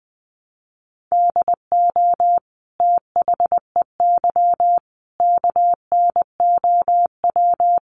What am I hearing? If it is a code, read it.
Morse code, heard as a keyed tone, about 20 wpm, 700 Hz. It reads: DO THEY KNOW
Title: Sentences from Top 100 Words 20wpm